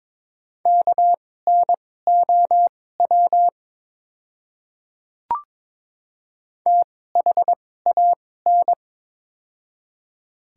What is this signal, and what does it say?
Morse code, heard as a keyed tone, about 22 wpm, spaced as for 16 wpm, 700 Hz. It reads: KNOW THAN
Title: 2x Character Spacing - Top 100 Words - 22wpm